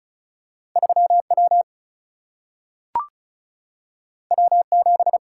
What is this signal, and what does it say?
Morse code, heard as a keyed tone, about 36 wpm, 700 Hz. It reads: 3 W W 7
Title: Single-Letter Single-Number 35wpm